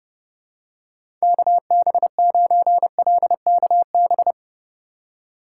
Morse code, heard as a keyed tone, about 32 wpm, 700 Hz. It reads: KB9LKB
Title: USA Call Signs - 6x Character Spacing - 30wpm